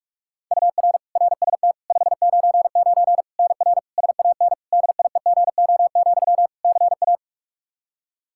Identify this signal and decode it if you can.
Morse code, heard as a keyed tone, about 46 wpm, 700 Hz. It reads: UR RST 599 NR SAN DIEGO, CA
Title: Practice QSO - Exchange 1 - Beginning - 45wpm